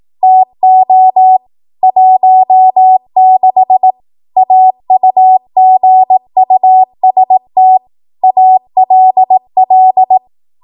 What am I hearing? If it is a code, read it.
Morse code, heard as a keyed tone, about 18 wpm, 750 Hz. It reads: TO 16 AUGUST ALL
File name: Morse / 18WPM